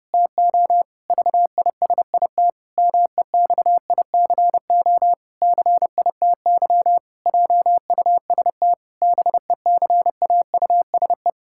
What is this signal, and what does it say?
Morse code, heard as a keyed tone, about 32 wpm, 700 Hz. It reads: TO VISIT MEXICO CITY JUST BECAUSE
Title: Pangram Challenge v2 - 30wpm